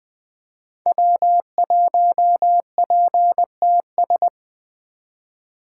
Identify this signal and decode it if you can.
Morse code, heard as a keyed tone, about 20 wpm, 700 Hz. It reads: W1PTS
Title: 2023 SOTA Activator Call Signs - 20wpm